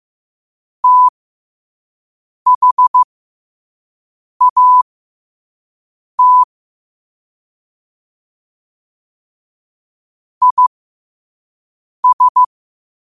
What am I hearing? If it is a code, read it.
Morse code, heard as a keyed tone, about 15 wpm, spaced as for 5 wpm, 1000 Hz. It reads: THAT IS
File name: qotdfiles / 5wpm